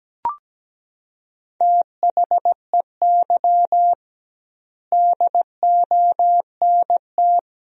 Morse code, heard as a keyed tone, about 17 wpm, 700 Hz. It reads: THEY DONT
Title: 2x Word-Spacing - Sentences from Top 300 Words - 17wpm